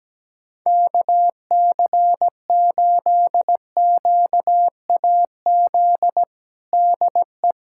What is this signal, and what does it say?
Morse code, heard as a keyed tone, about 17 wpm, 700 Hz. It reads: KC8QAZ DE
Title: USA Callsign DE Callsign - 6 Characters - 17wpm